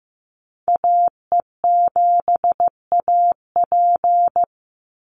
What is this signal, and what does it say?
Morse code, heard as a keyed tone, about 15 wpm, 700 Hz. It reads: AE7AP
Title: NA SOTA Activator Callsigns 15wpm